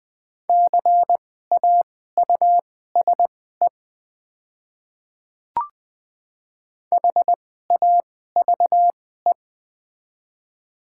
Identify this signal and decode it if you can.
Morse code, heard as a keyed tone, about 20 wpm, spaced as for 14 wpm, 690 Hz. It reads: CAUSE HAVE
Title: 2x Character Spacing - Top 500 Words - 20wpm